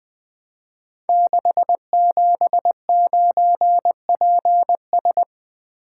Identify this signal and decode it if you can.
Morse code, heard as a keyed tone, about 20 wpm, 700 Hz. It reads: 679PS
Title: Code Groups - 5 Letter:Numbers - 20wpm